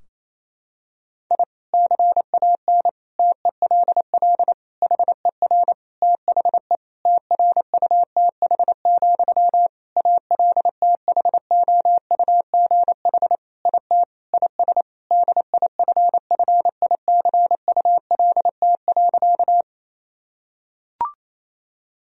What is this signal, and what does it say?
Morse code, heard as a keyed tone, about 28 wpm, 700 Hz. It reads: I CAN TELL HER THE TRUTH, ALTHOUGH IT IS DIFFICULT.
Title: Sentences from Top 1000 Words - Review Part 3 - 28wpm